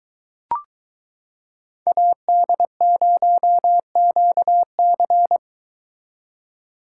Morse code, heard as a keyed tone, about 22 wpm, 700 Hz. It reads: AD0QC